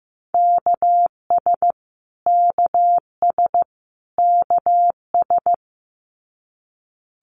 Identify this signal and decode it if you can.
Morse code, heard as a keyed tone, about 15 wpm, 700 Hz. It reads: KS KS KS
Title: US States Repeated 3x 15wpm